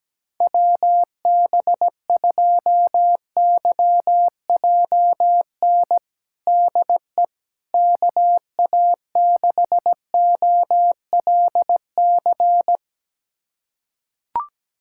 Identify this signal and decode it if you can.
Morse code, heard as a keyed tone, about 17 wpm, 700 Hz. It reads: WB2YJN DE KA6OLC
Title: USA Callsign DE Callsign - 6 Characters - 17wpm